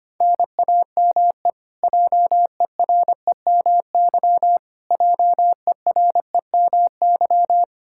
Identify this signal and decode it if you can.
Morse code, heard as a keyed tone, about 25 wpm, 700 Hz. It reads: NAME JEREMY JEREMY
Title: Practice QSO - Exchange 1 - Beginning - 25wpm